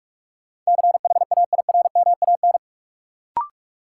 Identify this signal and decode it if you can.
Morse code, heard as a keyed tone, about 46 wpm, 700 Hz. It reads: CHAIRMAN
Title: Sets of 3 Words - New Word Drill - 45wpm